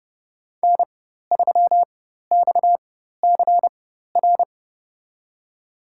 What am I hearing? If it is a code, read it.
Morse code, heard as a keyed tone, about 30 wpm, spaced as for 14 wpm, 700 Hz. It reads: N3XCR
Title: USA Call Signs - 5 Characters - 4x Character Spacing - 30wpm